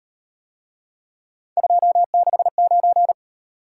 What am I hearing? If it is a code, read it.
Morse code, heard as a keyed tone, about 38 wpm, 700 Hz. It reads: 269